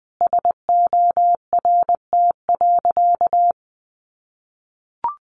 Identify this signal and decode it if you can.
Morse code, heard as a keyed tone, about 20 wpm, 700 Hz. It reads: SORT.
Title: Sentences from Top 1300 Words 20wpm